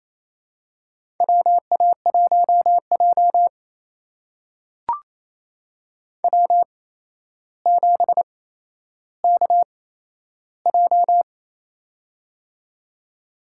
Morse code, heard as a keyed tone, about 27 wpm, spaced as for 12 wpm, 700 Hz. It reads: WA1J W7KJ
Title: USA Call Signs - 4 Characters - 8x Character Spacing - 28wpm